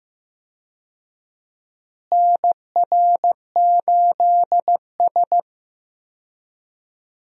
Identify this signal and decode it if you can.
Morse code, heard as a keyed tone, about 16 wpm, 700 Hz. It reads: NR8S
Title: USA Call Signs - 4 Characters - Repeated 2x - 15wpm